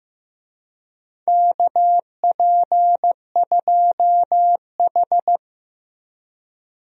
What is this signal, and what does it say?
Morse code, heard as a keyed tone, about 15 wpm, 700 Hz. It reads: KP2H
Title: DXCC-100 Call Signs - 4 Characters - Repeated 2x - 15wpm